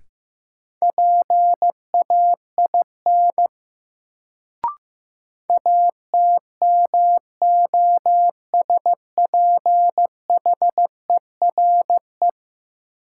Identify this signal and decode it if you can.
Morse code, heard as a keyed tone, about 15 wpm, 700 Hz. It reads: PAIN ATMOSPHERE
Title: Sentences from Top 1600 Words - New Word Drill Alt - 15wpm